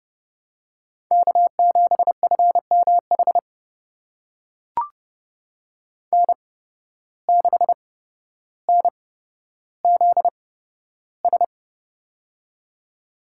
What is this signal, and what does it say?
Morse code, heard as a keyed tone, about 32 wpm, spaced as for 13 wpm, 700 Hz. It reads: K7FMH N6NZS
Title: USA Call Signs - 5 Characters - 8x Character Spacing - 30wpm